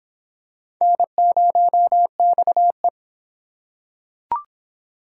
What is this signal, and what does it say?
Morse code, heard as a keyed tone, about 26 wpm, 700 Hz. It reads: N0XE